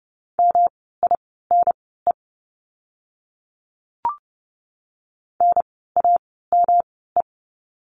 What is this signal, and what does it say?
Morse code, heard as a keyed tone, about 31 wpm, spaced as for 17 wpm, 700 Hz. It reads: MINE NAME
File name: course / 30:10wpm / 108 Lesson 18f - Letter m - Words 30wpm